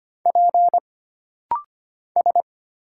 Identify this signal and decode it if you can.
Morse code, heard as a keyed tone, about 25 wpm, 700 Hz. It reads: P S
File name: Single Character (Letter:Number:Punctuation:Prosign) - Rapid-Fire - 25wpm